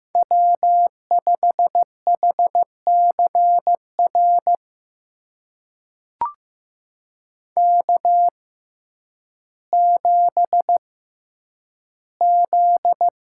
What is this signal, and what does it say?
Morse code, heard as a keyed tone, about 15 wpm, spaced as for 9 wpm, 700 Hz. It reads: W5HCR K7Z
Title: USA Call Signs - 5 Characters - 6x Character Spacing - 15wpm